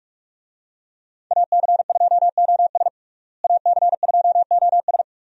Morse code, heard as a keyed tone, about 47 wpm, 700 Hz. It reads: AC2OS AC2OS
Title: USA Call Signs - 5 Characters - Repeated 2x - 45wpm